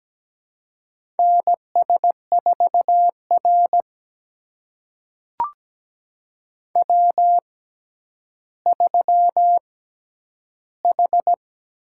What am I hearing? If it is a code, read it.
Morse code, heard as a keyed tone, about 17 wpm, spaced as for 10 wpm, 700 Hz. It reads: NS4R W3H
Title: USA Call Signs - 4 Characters - 6x Character Spacing - 17wpm